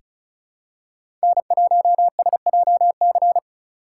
Morse code, heard as a keyed tone, about 36 wpm, 690 Hz. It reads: N1SJC
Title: USA Call Signs - 5 Characters - 3x Character Spacing - 35wpm